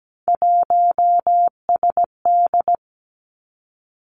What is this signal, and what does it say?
Morse code, heard as a keyed tone, about 17 wpm, 700 Hz. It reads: 1SD
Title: Three Challenging Letter-Number Combinations 17wpm